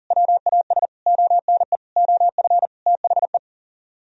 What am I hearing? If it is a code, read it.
Morse code, heard as a keyed tone, about 52 wpm, 700 Hz. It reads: WAS ONE OF THE